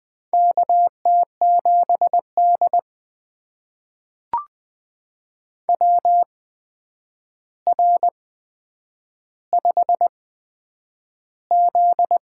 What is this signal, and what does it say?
Morse code, heard as a keyed tone, about 20 wpm, spaced as for 9 wpm, 700 Hz. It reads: KT7D WR5Z
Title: USA Call Signs - 4 Characters - 8x Character Spacing - 20wpm